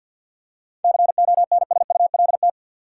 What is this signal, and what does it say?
Morse code, heard as a keyed tone, about 48 wpm, 700 Hz. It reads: CONSULT
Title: Seven Letter Words - 50wpm